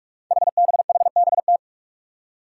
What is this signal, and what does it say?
Morse code, heard as a keyed tone, about 42 wpm, 700 Hz. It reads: HBHBT
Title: Five Challenging Letter-Number Combinations 45wpm